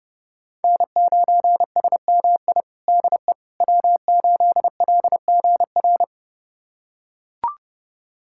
Morse code, heard as a keyed tone, about 30 wpm, 700 Hz. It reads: N9SMI DE W8LGR
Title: USA Callsign DE Callsign - 5 Characters - 30wpm